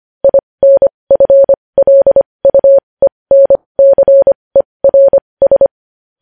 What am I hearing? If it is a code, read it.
Morse code, heard as a keyed tone, about 25 wpm, spaced as for 20 wpm, 550 Hz. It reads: INFLUENCERS
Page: news-headlines-in-morse-code-at-20-wpm